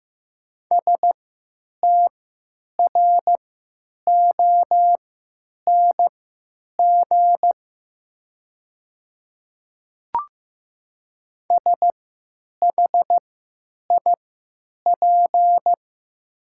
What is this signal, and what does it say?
Morse code, heard as a keyed tone, about 15 wpm, spaced as for 9 wpm, 700 Hz. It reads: STRONG SHIP
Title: 3x Character Spacing - Top 500 Words - 15wpm